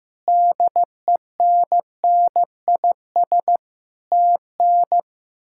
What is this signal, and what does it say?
Morse code, heard as a keyed tone, about 15 wpm, 700 Hz. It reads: DENNIS TN
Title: K1USN SST QSO Practice - 15wpm